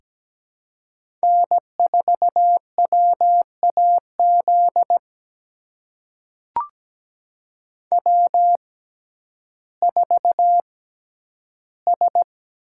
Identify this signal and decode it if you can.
Morse code, heard as a keyed tone, about 17 wpm, spaced as for 10 wpm, 700 Hz. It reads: N4WAZ W4S
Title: USA Call Signs - 5 Characters - 6x Character Spacing - 17wpm